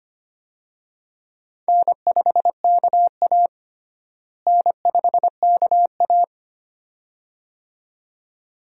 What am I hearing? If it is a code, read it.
Morse code, heard as a keyed tone, about 25 wpm, 700 Hz. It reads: N5KA N5KA
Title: USA Call Signs - 4 Characters - Repeated 2x - 25wpm